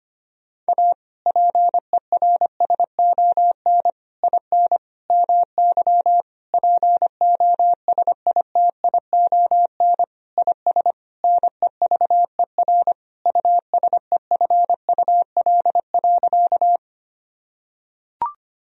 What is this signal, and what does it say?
Morse code, heard as a keyed tone, about 25 wpm, 700 Hz. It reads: A PERSON IN MY POSITION IS NEVER USEFUL.
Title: Sentences from Top 1000 Words - Review Part 4 - 25wpm